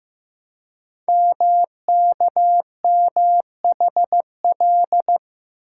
Morse code, heard as a keyed tone, about 15 wpm, 700 Hz. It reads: MKMHL
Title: Code Groups - 5 Letters - 15wpm